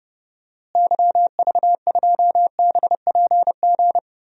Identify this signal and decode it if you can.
Morse code, heard as a keyed tone, about 30 wpm, 700 Hz. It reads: YV2BPG
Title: DXCC-100 Call Signs - 6 Characters - Repeated 2x - 30wpm